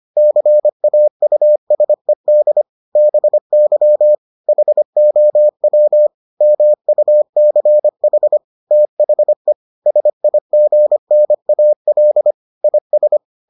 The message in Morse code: CAUSED BY HOW MUCH THE SIGNAL IS

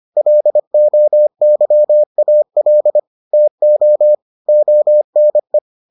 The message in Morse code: LOYAL TO ONE

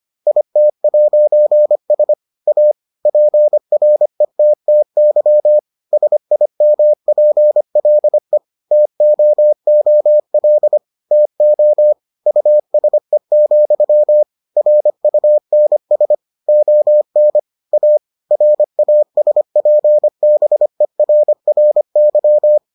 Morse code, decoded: IT'S A PRETTY SIMPLE TOOL TO USE, RUNS ON A RASPBERRY

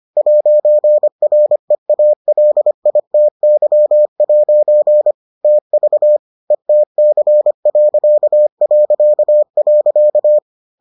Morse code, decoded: 'REALITY' TV ETC...

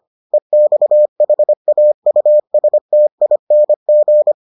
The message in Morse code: EXHAUSTING

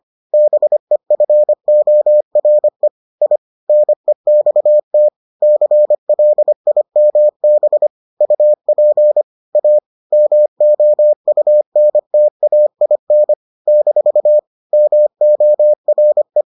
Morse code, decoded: BEFORE I NEXT CLIMB UP A MOUNTAIN - MORE

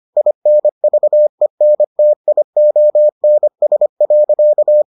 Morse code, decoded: INVENTIONS.

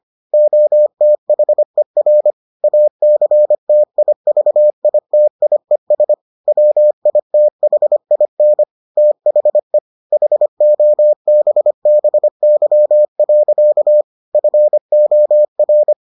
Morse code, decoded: OTHER ACTIVITIES WITHIN THE HOBBY. FOR

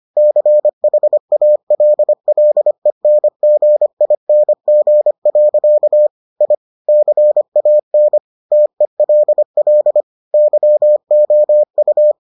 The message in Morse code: CHALLENGING. I CAN TELL YOU

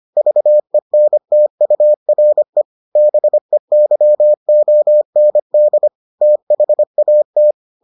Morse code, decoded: VENTURE BEYOND THAT

